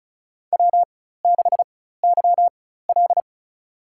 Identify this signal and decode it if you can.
Morse code, heard as a keyed tone, about 36 wpm, spaced as for 16 wpm, 700 Hz. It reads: W6YL
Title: USA Call Signs - 4 Characters - 4x Character Spacing - 35wpm